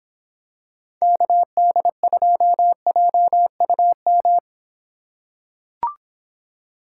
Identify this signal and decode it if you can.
Morse code, heard as a keyed tone, about 24 wpm, 700 Hz. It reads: KD2JUM